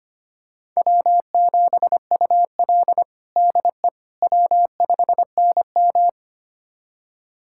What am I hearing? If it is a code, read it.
Morse code, heard as a keyed tone, about 25 wpm, 700 Hz. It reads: W7UL DE W5NM
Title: USA Callsign DE Callsign - 4 Characters - 25wpm